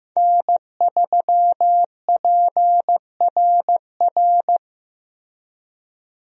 Morse code, heard as a keyed tone, about 16 wpm, 700 Hz. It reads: N3PRR
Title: USA Call Signs - 5 Characters - Repeated 3x - 15wpm